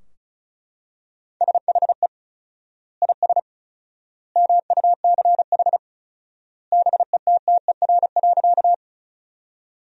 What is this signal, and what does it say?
Morse code, heard as a keyed tone, about 36 wpm, 700 Hz. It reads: SHE IS MUCH BETTER.
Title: 4x Word-Spacing - Sentences from Top 400 Words - 35wpm